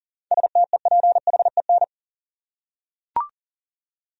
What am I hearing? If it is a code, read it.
Morse code, heard as a keyed tone, about 38 wpm, 700 Hz. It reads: STEPHEN
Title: USA Common Names - 40wpm